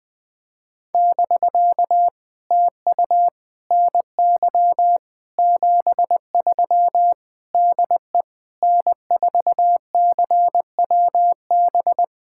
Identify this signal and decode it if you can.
Morse code, heard as a keyed tone, about 20 wpm, 700 Hz. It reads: <BK> TU NY 73 DE N4CWB
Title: POTA QSO Practice - 20wpm